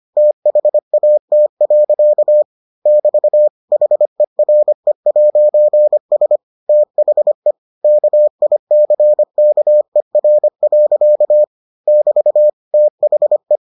Morse code, heard as a keyed tone, about 25 wpm, 600 Hz. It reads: THAT. <BT> HERE'S THE KICKER. <BT> THE